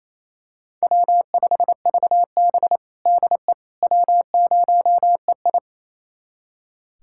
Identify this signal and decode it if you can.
Morse code, heard as a keyed tone, about 27 wpm, 700 Hz. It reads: W5VB DE W0EI
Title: USA Callsign DE Callsign - 4 Characters - 28wpm